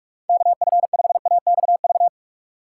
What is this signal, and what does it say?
Morse code, heard as a keyed tone, about 47 wpm, 700 Hz. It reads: KF5AXV
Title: USA Call Signs - 6 Characters - 6x Character Spacing - 45wpm